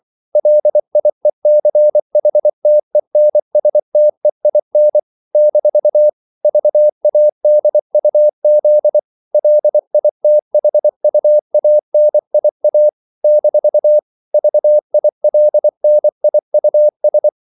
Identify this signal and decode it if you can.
Morse code, heard as a keyed tone, about 24 wpm, 600 Hz. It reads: LIECHTENSTEIN - VADUZ LITHUANIA - VILNIUS
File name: CW / Practice / NationalCapitals24